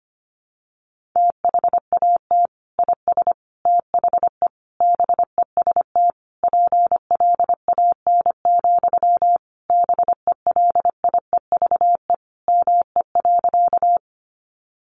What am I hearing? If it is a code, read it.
Morse code, heard as a keyed tone, about 25 wpm, 700 Hz. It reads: THAT IS THE BEST PLAN, BELIEVE ME.
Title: Sentences from Top 1000 Words 25wpm